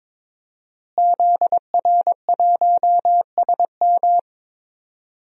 Morse code, heard as a keyed tone, about 22 wpm, 700 Hz. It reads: ZR1SM
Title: DXCC-100 Call Signs - 5 Characters - Repeated 3x - 22wpm